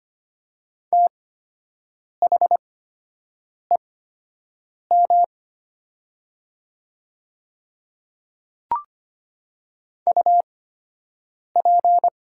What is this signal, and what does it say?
Morse code, heard as a keyed tone, about 25 wpm, spaced as for 7 wpm, 700 Hz. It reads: THEM UP
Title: 8x Character Spacing - Top 100 Words - 25wpm